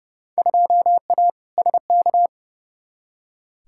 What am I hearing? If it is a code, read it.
Morse code, heard as a keyed tone, about 30 wpm, 700 Hz. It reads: 2A SK
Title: Field Day Exchanges Encore - 30wpm